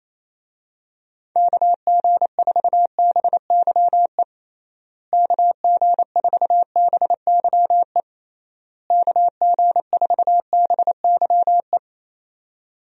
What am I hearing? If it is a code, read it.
Morse code, heard as a keyed tone, about 27 wpm, 700 Hz. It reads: KG4BYE KG4BYE KG4BYE
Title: USA Call Signs - 6 Characters - Repeated 3x - 28wpm